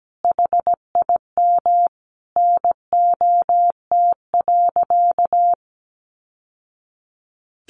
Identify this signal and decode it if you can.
Morse code, heard as a keyed tone, about 17 wpm, 700 Hz. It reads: HIM NOT.
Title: Sentences from Top 1200 Words 17wpm